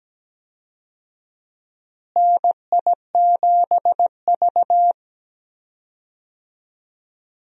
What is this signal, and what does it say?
Morse code, heard as a keyed tone, about 17 wpm, 700 Hz. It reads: NI7V
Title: USA Call Signs - 4 Characters - Repeated 3x - 17wpm